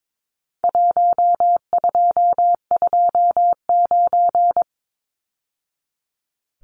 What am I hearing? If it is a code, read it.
Morse code, heard as a keyed tone, about 22 wpm, 700 Hz. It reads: 1229
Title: Numbers - Contest Style - 4-Digit 22wpm